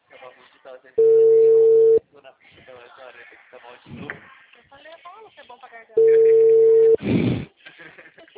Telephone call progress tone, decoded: ringback tone

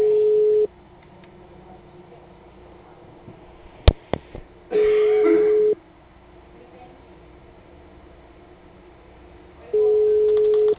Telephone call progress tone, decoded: ringback tone

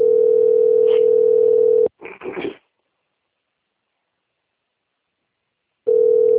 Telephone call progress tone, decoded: ringback tone

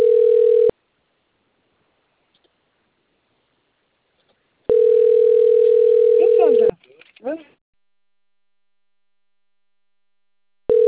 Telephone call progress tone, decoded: ringback tone

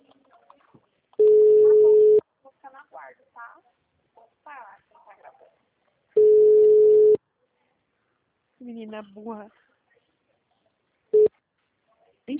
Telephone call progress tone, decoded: ringback tone

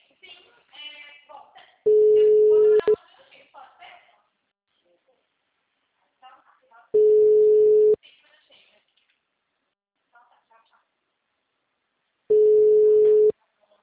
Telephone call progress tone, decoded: ringback tone

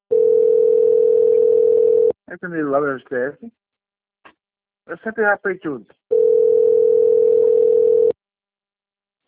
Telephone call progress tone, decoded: ringback tone

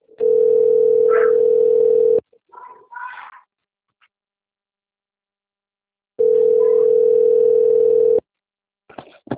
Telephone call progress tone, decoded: ringback tone

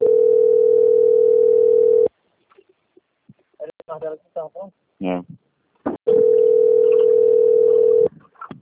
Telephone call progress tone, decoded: ringback tone

